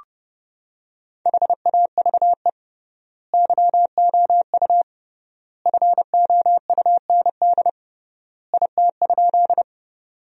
Morse code, heard as a keyed tone, about 32 wpm, 700 Hz. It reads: HAVE YOU FOUND IT?
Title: 3x Word-Spacing - Sentences from Top 300 Words - 30wpm